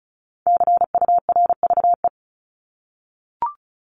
Morse code, heard as a keyed tone, about 34 wpm, 700 Hz. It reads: CURVE
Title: Top 1501-1750 Words 35wpm